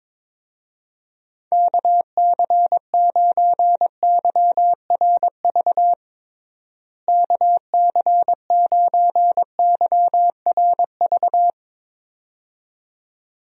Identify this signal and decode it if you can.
Morse code, heard as a keyed tone, about 22 wpm, 700 Hz. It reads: KC9YRV KC9YRV
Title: USA Call Signs - 6 Characters - Repeated 2x - 22wpm